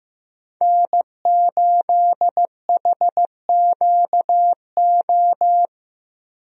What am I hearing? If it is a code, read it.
Morse code, heard as a keyed tone, about 15 wpm, 700 Hz. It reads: N8HQO
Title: USA Call Signs - 5 Characters - Repeated 3x - 15wpm